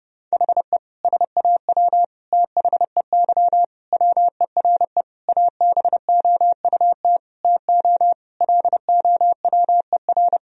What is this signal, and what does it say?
Morse code, heard as a keyed tone, about 30 wpm, 700 Hz. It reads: HE SAW THEY WERE ABOUT TO LOWER
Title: Sentences from Top 1000 Words - Review Part 4 - 30wpm